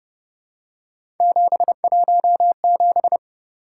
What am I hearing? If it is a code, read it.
Morse code, heard as a keyed tone, about 29 wpm, 700 Hz. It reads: 717
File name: Speed-Racing - Three Letter-Number Combinations - 25wpm